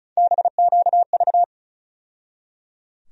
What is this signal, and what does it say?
Morse code, heard as a keyed tone, about 36 wpm, 700 Hz. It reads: BQV
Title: USA Call Sign Postfixes - 3 Letters 35wpm